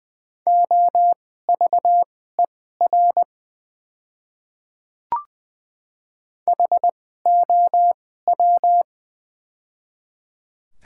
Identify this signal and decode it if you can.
Morse code, heard as a keyed tone, about 20 wpm, spaced as for 14 wpm, 690 Hz. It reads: OVER HOW
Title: 2x Character Spacing - Top 100 Words - 20wpm